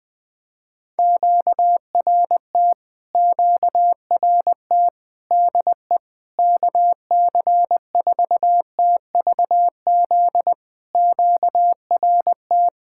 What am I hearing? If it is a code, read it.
Morse code, heard as a keyed tone, about 20 wpm, 700 Hz. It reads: QRT QRT DE KC4TVZ QRT
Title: POTA QSO Practice - 20wpm